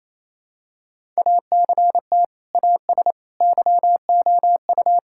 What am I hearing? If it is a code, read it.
Morse code, heard as a keyed tone, about 28 wpm, 700 Hz. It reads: ACT AS YOU
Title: Sentences from Top 1000 Words - Review Part 5 - 28wpm